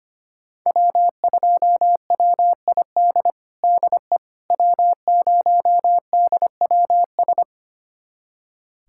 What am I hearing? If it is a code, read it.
Morse code, heard as a keyed tone, about 25 wpm, 700 Hz. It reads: W2WID DE W0DWS
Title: USA Callsign DE Callsign - 5 Characters - 25wpm